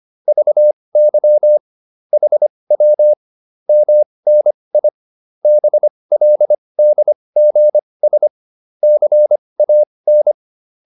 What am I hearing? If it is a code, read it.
Morse code, heard as a keyed tone, about 25 wpm, spaced as for 20 wpm, 600 Hz. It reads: VY HW MNI BLDGS CAN